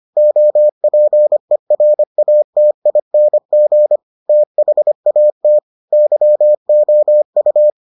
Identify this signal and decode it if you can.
Morse code, heard as a keyed tone, about 25 wpm, 600 Hz. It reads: OPERATING THAT YOU